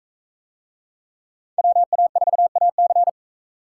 Morse code, heard as a keyed tone, about 37 wpm, 700 Hz. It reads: WA4AC